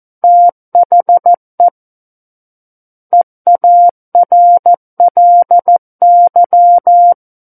Morse code, heard as a keyed tone, about 14 wpm, 700 Hz. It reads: THE EARLY
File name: Book One / 05-Chapter 5 - 14 WPM